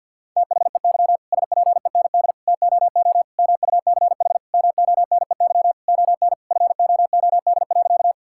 Morse code, heard as a keyed tone, about 50 wpm, 700 Hz. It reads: THEY SPEND TOO MUCH MONEY ON FOOD.
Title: Sentences from Top 1000 Words - Review Part 5 - 50wpm